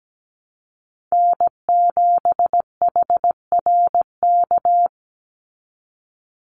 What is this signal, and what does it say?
Morse code, heard as a keyed tone, about 17 wpm, 700 Hz. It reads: N7HRK
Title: Call Signs Repeated 2x - 17wpm